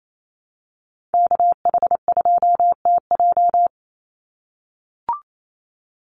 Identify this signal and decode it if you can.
Morse code, heard as a keyed tone, about 28 wpm, 700 Hz. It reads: KH2TJ
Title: NA SOTA Chaser Callsigns 28wpm